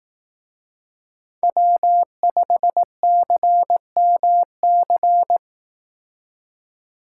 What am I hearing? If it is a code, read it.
Morse code, heard as a keyed tone, about 18 wpm, 700 Hz. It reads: W5CMC
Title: Speed Racing - USA Call Signs 15wpm